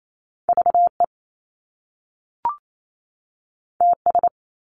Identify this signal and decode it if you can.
Morse code, heard as a keyed tone, about 28 wpm, 700 Hz. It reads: VE TS
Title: Top Two Letter Combinations in Words 28wpm